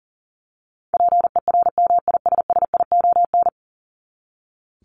Morse code, heard as a keyed tone, about 40 wpm, 700 Hz. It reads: PERMISSION
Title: Top 1501-1750 Words 40wpm